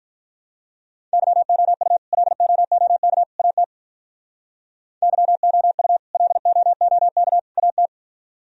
Morse code, heard as a keyed tone, about 46 wpm, 700 Hz. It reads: YOU LOOK AT YOU LOOK AT